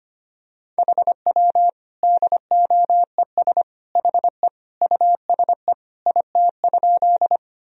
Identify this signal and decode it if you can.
Morse code, heard as a keyed tone, about 25 wpm, 700 Hz. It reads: HW DOES HE USE IT?
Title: Sentences from Top 200 Words and Abbreviations 25wpm